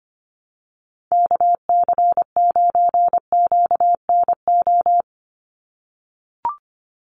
Morse code, heard as a keyed tone, about 25 wpm, 690 Hz. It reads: KC9QNO
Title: USA Call Signs 25wpm